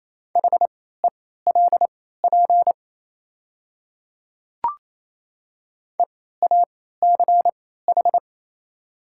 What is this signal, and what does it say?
Morse code, heard as a keyed tone, about 28 wpm, spaced as for 16 wpm, 700 Hz. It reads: HELP EACH
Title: 3x Character Spacing - Top 300 Words - 28wpm